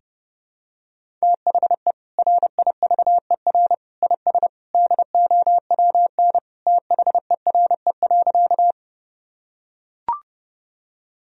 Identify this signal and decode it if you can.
Morse code, heard as a keyed tone, about 30 wpm, 700 Hz. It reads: THE RIVER IS DOWN THERE.
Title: Sentences from Top 1000 Words - Review Part 1 - 30wpm